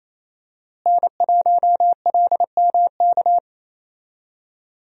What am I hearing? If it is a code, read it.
Morse code, heard as a keyed tone, about 28 wpm, 700 Hz. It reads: N1LMK
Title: USA Call Signs - 5 Characters - Repeated 2x - 28wpm